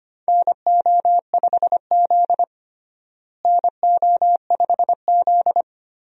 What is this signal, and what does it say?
Morse code, heard as a keyed tone, about 25 wpm, 700 Hz. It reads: NO5Z NO5Z
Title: USA Call Signs - 4 Characters - Repeated 3x - 25wpm